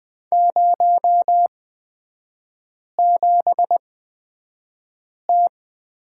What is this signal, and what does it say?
Morse code, heard as a keyed tone, about 20 wpm, spaced as for 5 wpm, 700 Hz. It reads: 07T
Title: 255 Lesson 41e - Period - Three Characters 20wpm